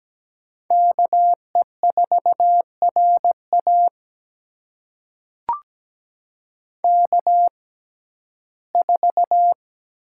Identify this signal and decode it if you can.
Morse code, heard as a keyed tone, about 17 wpm, spaced as for 12 wpm, 700 Hz. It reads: KE4RA K4
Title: USA Call Signs - 5 Characters - 6x Character Spacing - 17wpm